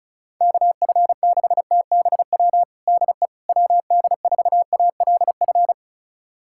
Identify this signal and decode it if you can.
Morse code, heard as a keyed tone, about 34 wpm, 700 Hz. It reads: KF6TBW DE WD4ALF
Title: USA Callsign DE Callsign - 6 Characters - 35wpm